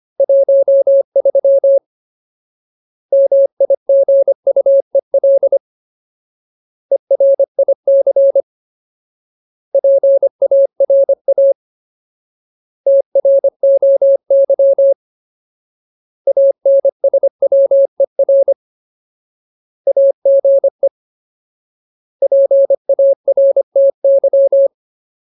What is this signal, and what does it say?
Morse code, heard as a keyed tone, about 25 wpm, 550 Hz. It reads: 13 MIGUEL ERIC PARA TROY ANSWER AGE PARTY
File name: Ultimate Word List / 25WPM Ultimate